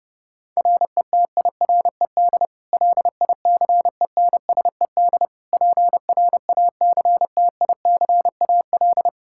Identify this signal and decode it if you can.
Morse code, heard as a keyed tone, about 32 wpm, 700 Hz. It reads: RETIRED LICENSED PRACTICAL
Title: Practice QSO - Exchange 3 - More Info - 30wpm